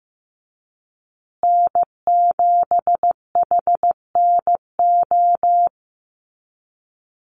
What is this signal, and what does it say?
Morse code, heard as a keyed tone, about 15 wpm, 700 Hz. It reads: N7HNO
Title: USA Call Signs Repeated 2x 15wpm